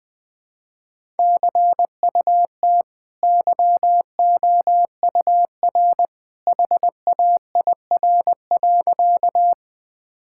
Morse code, heard as a keyed tone, about 20 wpm, 700 Hz. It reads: CUT YOUR HAIR.
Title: Easy Sentences with Hints - 20wpm